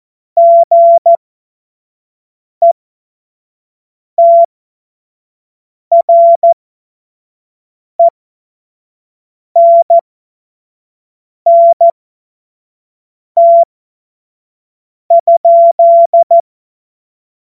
Morse code, heard as a keyed tone, about 14 wpm, spaced as for 5 wpm, 680 Hz. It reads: GETRENNT?
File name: Morsekurs / Witze / Witz 15. Februar 2022 25 F70